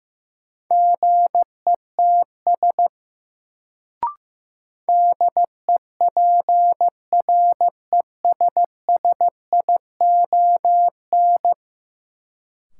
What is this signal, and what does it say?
Morse code, heard as a keyed tone, about 15 wpm, 700 Hz. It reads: GETS DEPRESSION
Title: Sentences from Top 1300 Words - New Word Drill - 15wpm